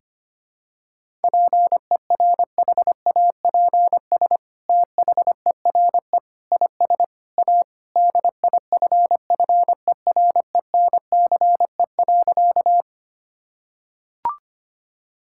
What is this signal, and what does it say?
Morse code, heard as a keyed tone, about 25 wpm, 700 Hz. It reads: PERHAPS THERE IS A DIFFERENCE.
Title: Sentences from Top 1000 Words - Review Part 5 - 25wpm